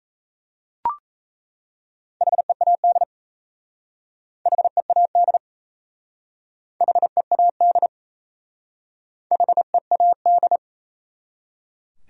Morse code, heard as a keyed tone, about 35 wpm, 700 Hz. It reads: HEAD HEAD HEAD HEAD
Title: Speed-Racing Top 500 Words 28wpm